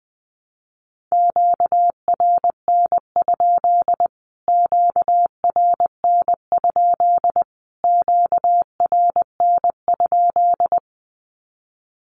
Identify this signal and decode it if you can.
Morse code, heard as a keyed tone, about 20 wpm, 700 Hz. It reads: QRN? QRN? QRN?
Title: Q-Codes Repeated 3x 20wpm